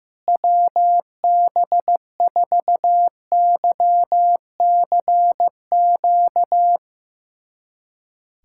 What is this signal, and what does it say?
Morse code, heard as a keyed tone, about 15 wpm, 700 Hz. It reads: WB4YCQ
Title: USA Callsign DE Callsign - 6 Characters - 15wpm